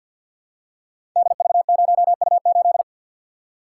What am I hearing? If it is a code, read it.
Morse code, heard as a keyed tone, about 52 wpm, 700 Hz. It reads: DV0U8
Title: Code Groups - 5 Letter:Numbers - 50wpm